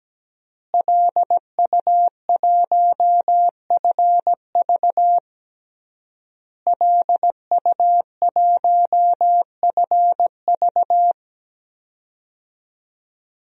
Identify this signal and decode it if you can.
Morse code, heard as a keyed tone, about 17 wpm, 700 Hz. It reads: LU1FV LU1FV
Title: DXCC-100 Call Signs - 5 Characters - Repeated 3x - 17wpm